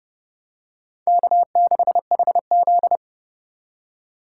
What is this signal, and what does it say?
Morse code, heard as a keyed tone, about 29 wpm, 700 Hz. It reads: K6HZ